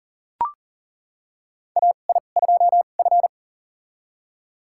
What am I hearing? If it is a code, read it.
Morse code, heard as a keyed tone, about 37 wpm, spaced as for 28 wpm, 700 Hz. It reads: AI2F
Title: USA Call Signs - 4 Characters - 2x Character Spacing - 40wpm